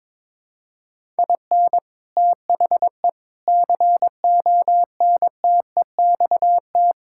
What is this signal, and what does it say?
Morse code, heard as a keyed tone, about 22 wpm, 700 Hz. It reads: IN THE CONTEXT
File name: practice / 22wpm / Build A Phrase - 5-Word Phrases - 22wpm